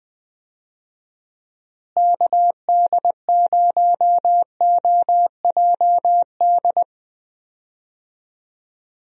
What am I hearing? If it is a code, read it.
Morse code, heard as a keyed tone, about 20 wpm, 700 Hz. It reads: KD0OJD
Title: USA Call Signs - 6 Characters - Repeated 2x - 20wpm